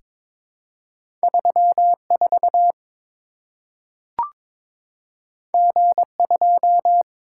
Morse code, heard as a keyed tone, about 22 wpm, 700 Hz. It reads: 34 G2
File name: practice / 22wpm / Two Letter-Number Combinations - 22wpm